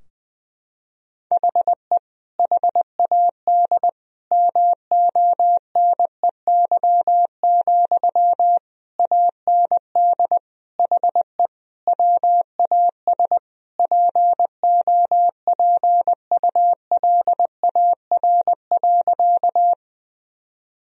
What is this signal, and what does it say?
Morse code, heard as a keyed tone, about 20 wpm, 700 Hz. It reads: HE HAD MONEY, AND HE WAS POPULAR.
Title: Sentences from Top 1000 Words - Review Part 3 - 20wpm